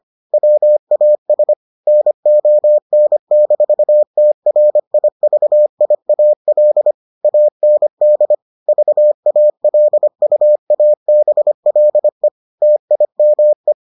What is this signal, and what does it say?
Morse code, heard as a keyed tone, about 25 wpm, 600 Hz. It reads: WAS NON-TRIVIAL AND VALUABLE TIME